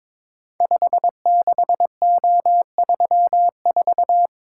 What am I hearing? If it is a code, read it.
Morse code, heard as a keyed tone, about 22 wpm, 690 Hz. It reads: 56O34
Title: Code Groups - 5 Letter:Numbers - 22wpm